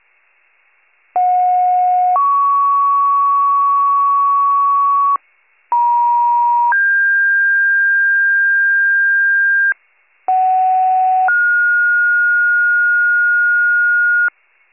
Two-tone page, heard as a one-second tone, then a three-second tone, about 730 then 1120 Hz, about 950 then 1600 Hz, about 750 then 1430 Hz, with noise. It is heard from 1.2 to 14.3 s.